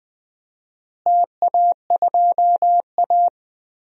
Morse code, heard as a keyed tone, about 21 wpm, 700 Hz. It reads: TA2A
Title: DXCC-100 Call Signs - 4 Characters - Repeated 2x - 20wpm